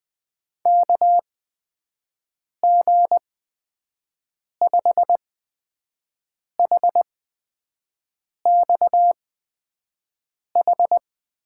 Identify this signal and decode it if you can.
Morse code, heard as a keyed tone, about 20 wpm, spaced as for 5 wpm, 700 Hz. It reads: KG5HXH
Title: USA Call Signs - 6 Characters - 8x Character Spacing - 20wpm